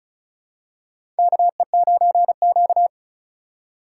Morse code, heard as a keyed tone, about 36 wpm, 700 Hz. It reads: KE9Q
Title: USA Call Signs - 8x Character Spacing - 35wpm